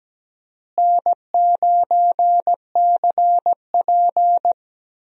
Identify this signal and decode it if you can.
Morse code, heard as a keyed tone, about 17 wpm, 700 Hz. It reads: N9CP
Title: USA Call Signs - 4 Characters - Repeated 2x - 17wpm